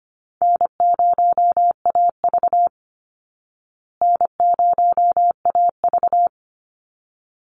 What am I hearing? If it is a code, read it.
Morse code, heard as a keyed tone, about 25 wpm, 700 Hz. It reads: N0AV N0AV
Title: Easy Call Signs Repeated 2x - 25wpm